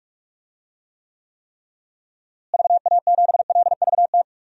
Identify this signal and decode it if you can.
Morse code, heard as a keyed tone, about 42 wpm, 700 Hz. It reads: VA7LVT
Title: DXCC-100 Call Signs - 6 Characters - Repeated 3x - 45wpm